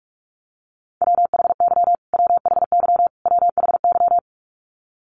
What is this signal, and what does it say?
Morse code, heard as a keyed tone, about 44 wpm, 700 Hz. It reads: WHY WHY WHY
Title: Top 300 Words Repeated 45wpm